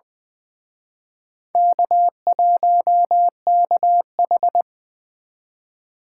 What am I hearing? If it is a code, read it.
Morse code, heard as a keyed tone, about 24 wpm, 700 Hz. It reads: K1KH
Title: Speed Racing - USA 1x2 Call Signs 20wpm